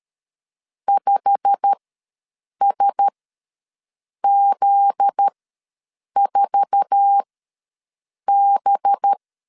Morse code, heard as a keyed tone, about 13 wpm, spaced as for 6 wpm, 800 Hz. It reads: 5SZ4B
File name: LEZIONI / 9 286 ws 12wpm 3351